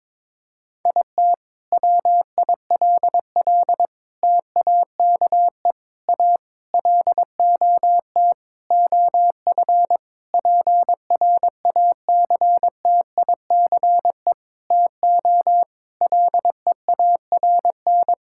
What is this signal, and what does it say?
Morse code, heard as a keyed tone, about 22 wpm, 700 Hz. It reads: IT WILL TAKE A LOT OF PRACTICE TO LEARN
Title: Sentences from Top 1000 Words - Review Part 6 - 22wpm